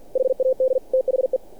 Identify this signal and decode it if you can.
Morse code, heard as a keyed tone, about 48 wpm, 520 Hz. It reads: HAD THE
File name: archived / CRSnet